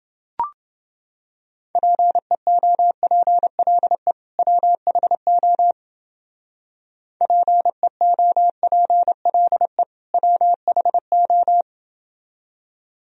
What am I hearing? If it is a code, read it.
Morse code, heard as a keyed tone, about 30 wpm, 700 Hz. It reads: PEOPLE WHO PEOPLE WHO